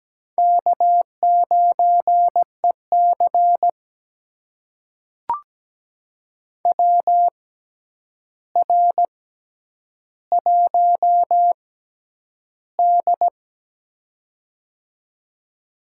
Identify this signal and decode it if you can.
Morse code, heard as a keyed tone, about 17 wpm, spaced as for 9 wpm, 700 Hz. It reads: K9EC WR1D
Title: USA Call Signs - 4 Characters - 6x Character Spacing - 17wpm